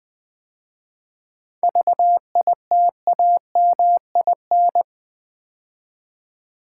Morse code, heard as a keyed tone, about 20 wpm, 700 Hz. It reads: VITAMIN
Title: 7 Letter Words - First 3 Letters Given - 20wpm